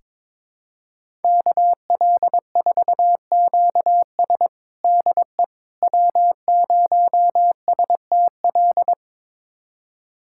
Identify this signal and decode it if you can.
Morse code, heard as a keyed tone, about 22 wpm, 700 Hz. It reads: KL4QS DE W0STL
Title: USA Callsign DE Callsign - 5 Characters - 22wpm